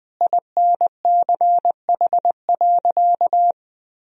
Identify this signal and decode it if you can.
Morse code, heard as a keyed tone, about 20 wpm, 700 Hz. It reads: INCH.
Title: Sentences from Top 1000 Words - Review Part 3 - 20wpm